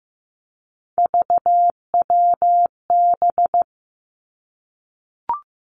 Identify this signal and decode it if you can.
Morse code, heard as a keyed tone, about 15 wpm, 700 Hz. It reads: VWB
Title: Three Challenging Letter-Number Combinations 15wpm